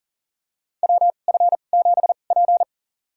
Morse code, heard as a keyed tone, about 37 wpm, spaced as for 28 wpm, 700 Hz. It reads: WF7P
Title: USA Call Signs - 2x Character Spacing - 40wpm